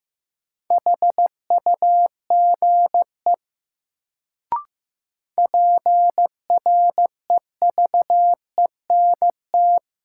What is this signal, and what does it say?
Morse code, heard as a keyed tone, about 15 wpm, 700 Hz. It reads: HUGE PREVENT
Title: Sentences from Top 1100 Words - New Word Drill Alt - 15wpm